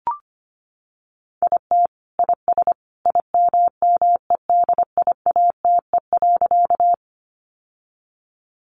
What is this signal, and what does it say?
Morse code, heard as a keyed tone, about 25 wpm, 700 Hz. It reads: IT IS IMMEDIATE.
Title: Sentences from Top 1200 Words 25wpm